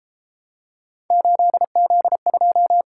Morse code, heard as a keyed tone, about 32 wpm, 700 Hz. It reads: 8Z2